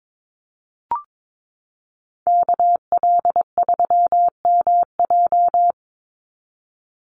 Morse code, heard as a keyed tone, about 22 wpm, 700 Hz. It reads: KL3MJ
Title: USA Call Signs 22wpm